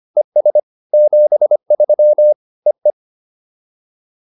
Morse code, heard as a keyed tone, about 25 wpm, 600 Hz. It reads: ES 73 EE